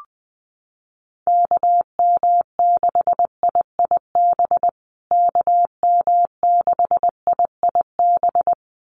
Morse code, heard as a keyed tone, about 20 wpm, 700 Hz. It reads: KM6IIB KM6IIB
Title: USA Call Signs Repeated 2x 20wpm